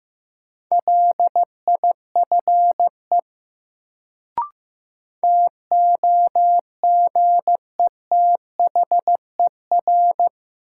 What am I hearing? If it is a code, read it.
Morse code, heard as a keyed tone, about 16 wpm, 700 Hz. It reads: LIFE TOGETHER
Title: Sentences from Top 300 Words - New Word Drill - 15wpm